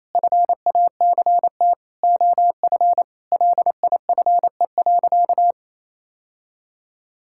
Morse code, heard as a keyed tone, about 28 wpm, 700 Hz. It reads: FACT OF LIFE.
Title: Sentences from Top 1000 Words - Review Part 5 - 28wpm